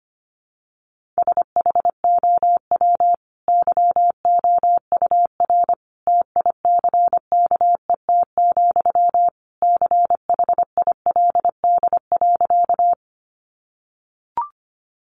Morse code, heard as a keyed tone, about 25 wpm, 700 Hz. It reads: SHOW YOUR TICKET, CHILD.
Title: Sentences from Top 1400 Words 25wpm